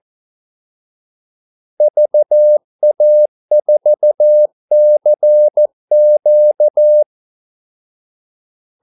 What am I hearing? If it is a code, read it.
Morse code, heard as a keyed tone, about 14 wpm, 600 Hz. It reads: VA4CQ
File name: Callsigns-14wpm